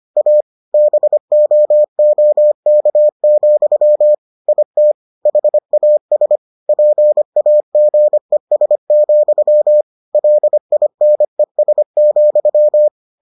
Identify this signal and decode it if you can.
Morse code, heard as a keyed tone, about 25 wpm, 600 Hz. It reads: A BOOK, IT HAS PAGES, LINES,